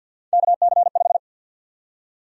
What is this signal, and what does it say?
Morse code, heard as a keyed tone, about 48 wpm, 700 Hz. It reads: KC5
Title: USA Call Sign Prefixes - Repeated 2x - 50wpm